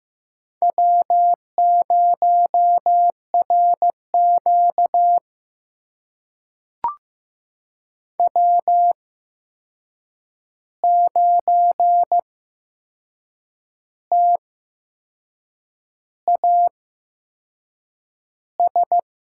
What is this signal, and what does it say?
Morse code, heard as a keyed tone, about 15 wpm, spaced as for 6 wpm, 700 Hz. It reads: W0RQ W9TAS
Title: USA Call Signs - 8x Character Spacing - 15wpm